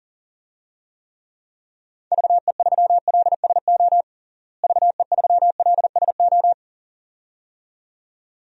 Morse code, heard as a keyed tone, about 40 wpm, 700 Hz. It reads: VE3LSO VE3LSO
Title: DXCC-100 Call Signs - 6 Characters - Repeated 2x - 40wpm